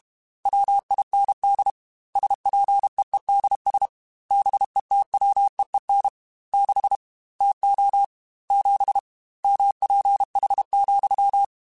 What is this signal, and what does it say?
Morse code, heard as a keyed tone, about 32 wpm, 800 Hz. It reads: WIND SPEEDS BETWEEN 6 TO 7 MPH,
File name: Weather Sets / Weather Sets-mp3s / Weather 4 24 @30